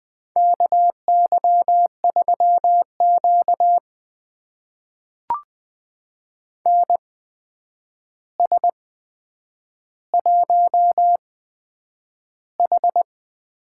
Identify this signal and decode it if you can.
Morse code, heard as a keyed tone, about 20 wpm, spaced as for 9 wpm, 700 Hz. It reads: KY3Q NS1H
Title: USA Call Signs - 4 Characters - 8x Character Spacing - 20wpm